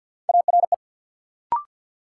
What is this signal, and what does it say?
Morse code, heard as a keyed tone, about 48 wpm, 700 Hz. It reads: ARE
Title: Sentences from Top 100 Words - New Word Drill - 50wpm